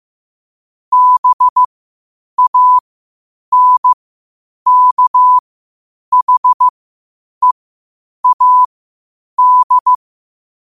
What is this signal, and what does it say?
Morse code, heard as a keyed tone, about 15 wpm, spaced as for 8 wpm, 1000 Hz. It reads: BANKHEAD
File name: qotdfiles / 8wpm